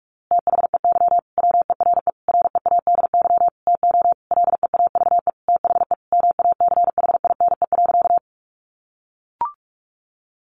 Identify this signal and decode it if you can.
Morse code, heard as a keyed tone, about 44 wpm, 700 Hz. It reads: THEY WERE READY TO LEAVE THE MACHINE.
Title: A Sound Of Thunder 45wpm